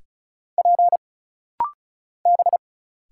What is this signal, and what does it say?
Morse code, heard as a keyed tone, about 34 wpm, 700 Hz. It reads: P B
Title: Single Letters - Rapid-Fire 35wpm